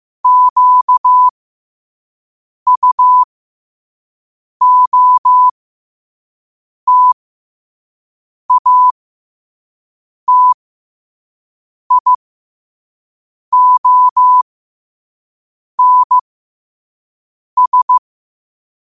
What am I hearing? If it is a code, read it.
Morse code, heard as a keyed tone, about 15 wpm, spaced as for 5 wpm, 1000 Hz. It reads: QUOTATIONS